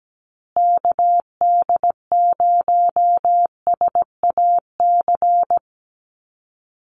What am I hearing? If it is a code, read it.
Morse code, heard as a keyed tone, about 17 wpm, 700 Hz. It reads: KD0SAC
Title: USA Call Signs Repeated 2x 17wpm